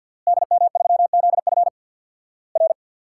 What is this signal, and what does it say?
Morse code, heard as a keyed tone, about 50 wpm, 690 Hz. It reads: DM3ZF R
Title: Worked All Europe DX Contest - QTC Exchange with Repeats - 50wpm